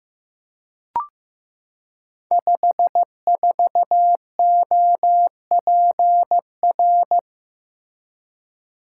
Morse code, heard as a keyed tone, about 15 wpm, 700 Hz. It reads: 54OPR
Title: Code Groups - 5 Letter:Numbers - 15wpm